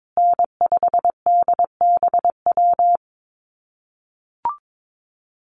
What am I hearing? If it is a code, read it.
Morse code, heard as a keyed tone, about 22 wpm, 700 Hz. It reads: N5DBW
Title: USA Call Signs Repeated 2x 22wpm